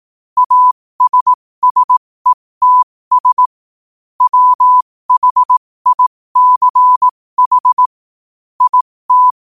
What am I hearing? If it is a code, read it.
Morse code, heard as a keyed tone, about 18 wpm, spaced as for 15 wpm, 1000 Hz. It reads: ASSETS WHICH IT
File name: qotdfiles / 15wpm